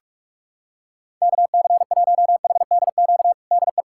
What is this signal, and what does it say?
Morse code, heard as a keyed tone, about 46 wpm, 700 Hz. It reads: KC1HDQ DE
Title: USA Callsign DE Callsign - 6 Characters - 45wpm